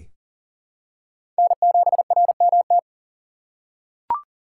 Morse code, heard as a keyed tone, about 40 wpm, 700 Hz. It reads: N7RMT
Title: USA Call Signs - 5 Characters - 3x Character Spacing - 40wpm